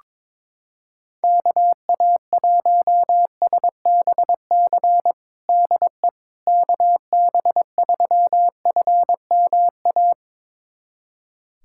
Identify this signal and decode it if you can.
Morse code, heard as a keyed tone, about 22 wpm, 700 Hz. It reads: KA1SBC DE KB3FMA
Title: USA Callsign DE Callsign - 6 Characters - 22wpm